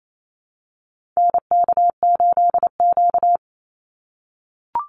Morse code, heard as a keyed tone, about 28 wpm, 700 Hz. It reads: NK8Q
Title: NA SOTA Activator Callsigns 28wpm